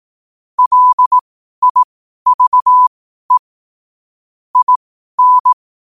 Morse code, heard as a keyed tone, about 18 wpm, spaced as for 12 wpm, 1000 Hz. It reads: LIVE IN
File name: qotdfiles / 12wpm